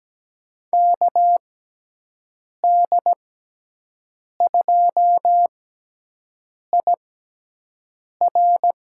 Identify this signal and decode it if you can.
Morse code, heard as a keyed tone, about 17 wpm, spaced as for 6 wpm, 700 Hz. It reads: KD2IR
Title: USA Call Signs - 6 Characters - 6x Character Spacing - 17wpm